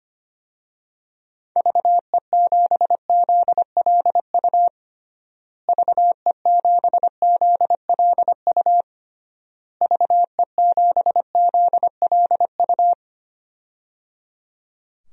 Morse code, heard as a keyed tone, about 25 wpm, 700 Hz. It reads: VE7ZLU VE7ZLU VE7ZLU
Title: DXCC-100 Call Signs - 6 Characters - Repeated 3x - 25wpm